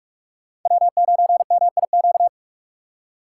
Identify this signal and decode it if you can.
Morse code, heard as a keyed tone, about 42 wpm, 700 Hz. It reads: W9MIQ
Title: USA Call Signs - 5 Characters - 2x Character Spacing - 45wpm